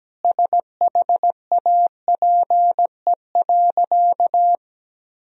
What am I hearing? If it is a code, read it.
Morse code, heard as a keyed tone, about 17 wpm, 700 Hz. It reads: SHAPE.
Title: Sentences from Top 1000 Words - Review Part 3 - 17wpm